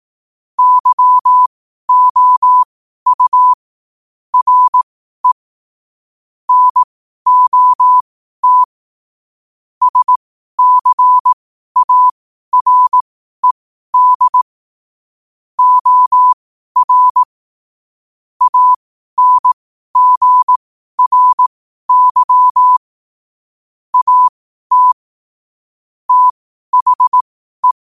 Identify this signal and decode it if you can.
Morse code, heard as a keyed tone, about 18 wpm, spaced as for 12 wpm, 1000 Hz. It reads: YOU RE NOT SCARED OR ANGRY AT THE
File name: qotdfiles / 12wpm